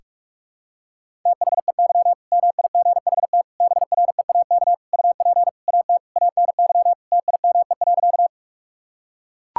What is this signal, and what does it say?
Morse code, heard as a keyed tone, about 41 wpm, 700 Hz. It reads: THEY MIGHT BREAK UP AT ANY TIME.
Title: Sentences from Top 1000 Words - Review Part 6 - 45wpm